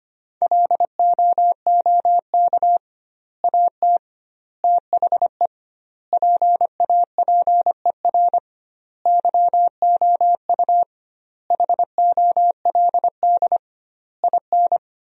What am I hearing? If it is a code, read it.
Morse code, heard as a keyed tone, about 25 wpm, 700 Hz. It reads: LOOK AT THE PAPER YOU HOLD IN
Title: 2x Word-Spacing - Sentences from Top 400 Words - 25wpm